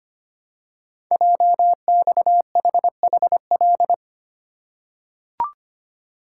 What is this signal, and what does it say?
Morse code, heard as a keyed tone, about 25 wpm, 700 Hz. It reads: JXHHL
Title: Code Groups - 5 Letters - 25wpm